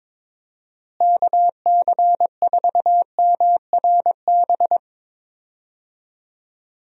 Morse code, heard as a keyed tone, about 22 wpm, 700 Hz. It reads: KC4MRB
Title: USA Call Signs - 6 Characters - Repeated 2x - 22wpm